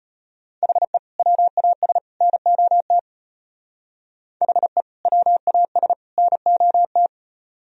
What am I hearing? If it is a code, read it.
Morse code, heard as a keyed tone, about 33 wpm, 700 Hz. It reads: HE WAS NOT HE WAS NOT